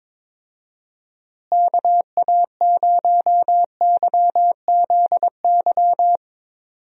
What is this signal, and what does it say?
Morse code, heard as a keyed tone, about 22 wpm, 700 Hz. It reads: KA0YZY
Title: USA Call Signs - 6 Characters - Repeated 3x - 22wpm